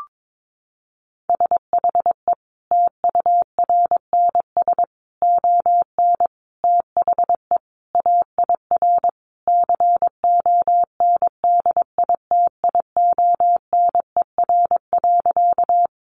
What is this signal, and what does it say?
Morse code, heard as a keyed tone, about 22 wpm, 700 Hz. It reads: SHE TURNS ON THE AIR CONDITIONER.
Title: Easy Phrases 22wpm